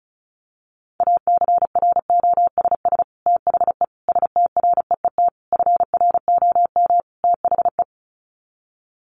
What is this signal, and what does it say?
Morse code, heard as a keyed tone, about 34 wpm, 700 Hz. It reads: ACROSS THE STREET FROM THE
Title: Sets of 5 Words Encore 35wpm